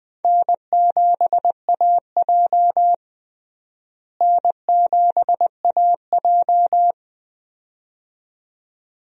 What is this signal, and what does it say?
Morse code, heard as a keyed tone, about 20 wpm, 700 Hz. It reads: N7AJ N7AJ
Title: USA Call Signs - 4 Characters - Repeated 2x - 20wpm